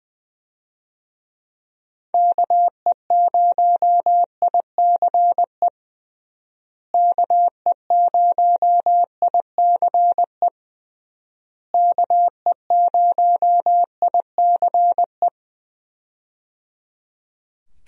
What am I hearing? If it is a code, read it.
Morse code, heard as a keyed tone, about 21 wpm, 700 Hz. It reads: KE0ICE KE0ICE KE0ICE
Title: USA Call Signs - 6 Characters - Repeated 3x - 20wpm